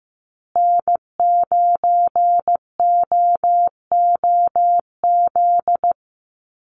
Morse code, heard as a keyed tone, about 15 wpm, 700 Hz. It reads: N9OOZ
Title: USA Call Signs Repeated 3x 15wpm